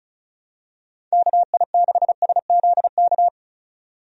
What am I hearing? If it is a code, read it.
Morse code, heard as a keyed tone, about 34 wpm, 700 Hz. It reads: KI6SZK
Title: USA Call Signs - 6 Characters - 35wpm